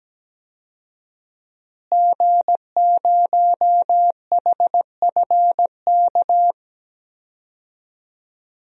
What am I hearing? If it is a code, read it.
Morse code, heard as a keyed tone, about 17 wpm, 700 Hz. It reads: G0HFK
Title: DXCC-100 Call Signs - 5 Characters - Repeated 3x - 17wpm